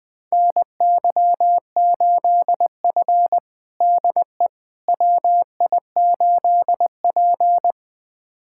Morse code, heard as a keyed tone, about 20 wpm, 700 Hz. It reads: NY8F DE WI8P
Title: USA Callsign DE Callsign - 4 Characters - 20wpm